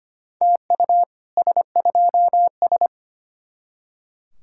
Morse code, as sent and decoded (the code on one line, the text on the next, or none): - ..- / ... ..--- ...
TU S2S